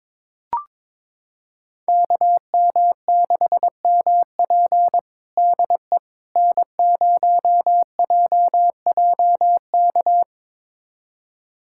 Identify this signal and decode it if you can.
Morse code, heard as a keyed tone, about 22 wpm, 700 Hz. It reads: KM6MP DE N0JJK
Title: USA Callsign DE Callsign - 5 Characters - 22wpm